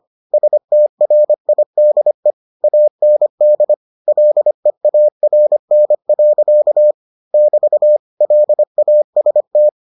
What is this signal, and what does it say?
Morse code, heard as a keyed tone, about 25 wpm, 600 Hz. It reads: STRIDE AND LEARN. <BT> LAST